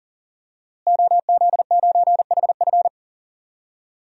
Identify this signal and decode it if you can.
Morse code, heard as a keyed tone, about 40 wpm, 700 Hz. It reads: OZ9HF
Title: DXCC-100 Call Signs - 5 Characters - Repeated 3x - 40wpm